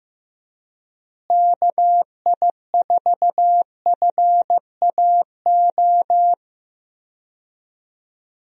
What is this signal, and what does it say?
Morse code, heard as a keyed tone, about 15 wpm, 700 Hz. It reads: KI4FAO
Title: USA Call Signs - 6 Characters - Repeated 3x - 15wpm